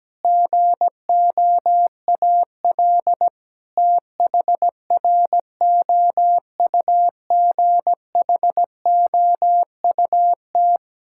Morse code, heard as a keyed tone, about 17 wpm, 700 Hz. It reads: GOAL THROUGHOUT
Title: Sentences from Top 1000 Words - Review Part 5 - 17wpm